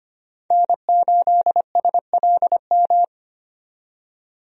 Morse code, heard as a keyed tone, about 25 wpm, 700 Hz. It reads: N8SLM
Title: USA Call Signs - 5 Characters - 25wpm